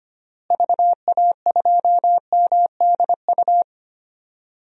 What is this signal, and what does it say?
Morse code, heard as a keyed tone, about 25 wpm, 700 Hz. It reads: VA2MDU
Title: DXCC-100 Call Signs - 6 Characters - Repeated 3x - 25wpm